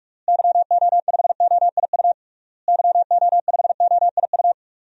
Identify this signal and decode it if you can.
Morse code, heard as a keyed tone, about 46 wpm, 700 Hz. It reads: YO5OIU YO5OIU
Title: DXCC-100 Call Signs - 6 Characters - Repeated 3x - 45wpm